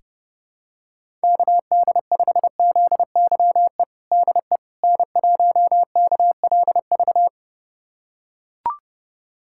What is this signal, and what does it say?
Morse code, heard as a keyed tone, about 30 wpm, 700 Hz. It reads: KD5ZYE DE N1KLV
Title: USA Callsign DE Callsign 30wpm